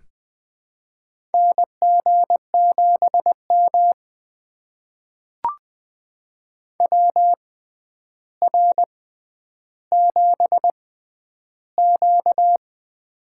Morse code, heard as a keyed tone, about 20 wpm, spaced as for 10 wpm, 700 Hz. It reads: NG7M WR7Q
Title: USA Call Signs - 4 Characters - 6x Character Spacing - 20wpm